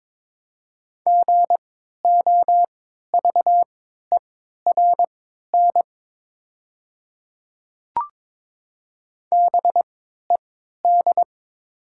Morse code, heard as a keyed tone, about 22 wpm, spaced as for 12 wpm, 700 Hz. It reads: GOVERN BED
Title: 3x Character Spacing - Top 500 Words - 22wpm